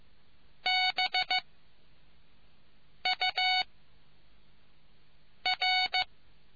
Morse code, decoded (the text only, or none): B U R